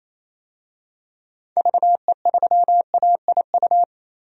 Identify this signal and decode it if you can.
Morse code, heard as a keyed tone, about 28 wpm, 700 Hz. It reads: VE3AIU
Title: DXCC-100 Call Signs - 6 Characters - Repeated 2x - 28wpm